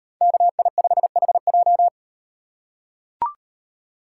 Morse code, heard as a keyed tone, about 36 wpm, 700 Hz. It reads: KI5HJ